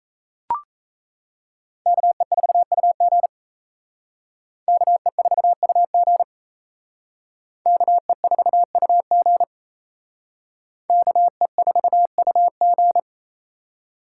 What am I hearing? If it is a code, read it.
Morse code, heard as a keyed tone, about 37 wpm, 700 Hz. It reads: KE4UG KE4UG KE4UG KE4UG